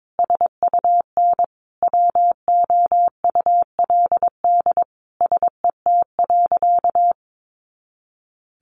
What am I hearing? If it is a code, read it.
Morse code, heard as a keyed tone, about 22 wpm, 700 Hz. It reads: SUN WOULD SET.
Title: Sentences from Top 400 Words 22wpm